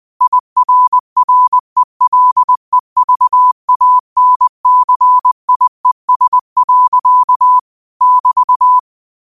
Morse code, heard as a keyed tone, about 20 wpm, 1000 Hz. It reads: IRRELEVANCIES. <BT>